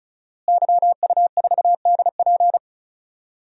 Morse code, heard as a keyed tone, about 36 wpm, 700 Hz. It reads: YU4DP
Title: DXCC-100 Call Signs - 5 Characters - Repeated 2x - 35wpm